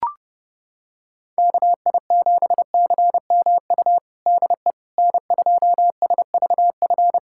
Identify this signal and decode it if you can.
Morse code, heard as a keyed tone, about 30 wpm, 700 Hz. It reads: KI7CMU DE N2SVF
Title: USA Callsign DE Callsign 30wpm